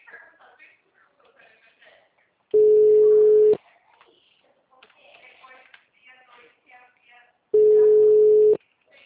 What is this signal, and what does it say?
Ringback tone (Europe, 425 Hz), with speech.
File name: out- -2019- - - .12342